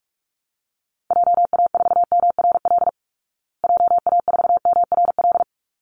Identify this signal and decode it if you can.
Morse code, heard as a keyed tone, about 44 wpm, 700 Hz. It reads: JA4MRL JA4MRL
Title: Call Signs Repeated 3x - 45wpm